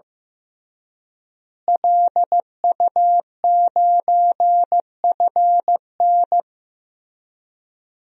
Morse code, heard as a keyed tone, about 15 wpm, 700 Hz. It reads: LU9FN
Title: DXCC-100 Call Signs - 5 Characters - Repeated 3x - 15wpm